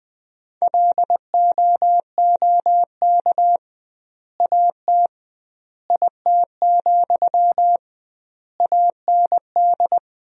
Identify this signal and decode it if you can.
Morse code, heard as a keyed tone, about 20 wpm, 700 Hz. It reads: LOOK AT IT, AND
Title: 2x Word-Spacing - Sentences from Top 200 Words - 20wpm